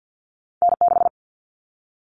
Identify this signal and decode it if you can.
Morse code, heard as a keyed tone, about 50 wpm, 700 Hz. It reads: N6
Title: Callsign Prefixes for Top 25 DXCC - 50wpm